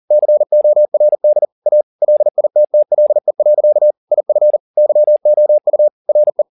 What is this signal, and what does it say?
Morse code, heard as a keyed tone, about 40 wpm, 600 Hz. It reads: CORD A LITTLE. IF YOU RE